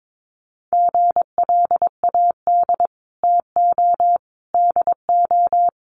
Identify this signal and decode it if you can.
Morse code, heard as a keyed tone, about 22 wpm, 700 Hz. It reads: GLAD TO DO
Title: Sentences from Top 800 Words 22wpm